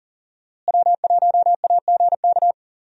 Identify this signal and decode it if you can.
Morse code, heard as a keyed tone, about 40 wpm, 700 Hz. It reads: W1AGK
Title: USA Call Signs - 5 Characters - Repeated 2x - 40wpm